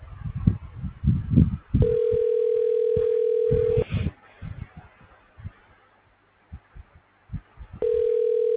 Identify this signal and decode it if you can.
Ringback tone (North America), 440 + 480 Hz.